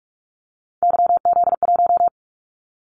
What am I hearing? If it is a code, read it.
Morse code, heard as a keyed tone, about 46 wpm, 700 Hz. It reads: YZ1
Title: Callsign Prefixes for Top 25 DXCC - 45wpm